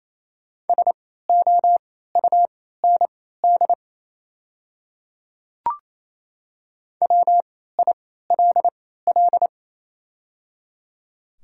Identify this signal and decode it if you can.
Morse code, heard as a keyed tone, about 28 wpm, spaced as for 16 wpm, 700 Hz. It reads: SOUND WILL
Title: 3x Character Spacing - Top 100 Words - 28wpm